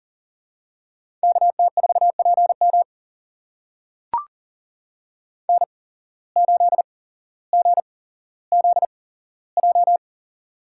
Morse code, heard as a keyed tone, about 42 wpm, spaced as for 17 wpm, 700 Hz. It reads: KT4PM N8GZJ
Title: USA Call Signs - 5 Characters - 8x Character Spacing - 40wpm